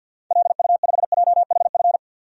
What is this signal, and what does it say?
Morse code, heard as a keyed tone, about 52 wpm, 700 Hz. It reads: LU5JHF
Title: DXCC-100 Call Signs - 6 Characters - 50wpm